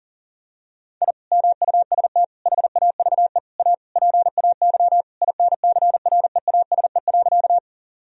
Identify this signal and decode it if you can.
Morse code, heard as a keyed tone, about 40 wpm, 700 Hz. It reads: I MUST HAVE A PAY INCREASE.
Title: Sentences from Top 1000 Words - Review Part 4 - 40wpm